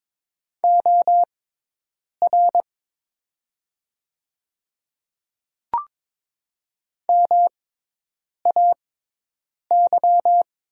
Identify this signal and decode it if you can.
Morse code, heard as a keyed tone, about 22 wpm, spaced as for 8 wpm, 700 Hz. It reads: OR MAY
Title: 6x Character Spacing - Top 100 Words - 22wpm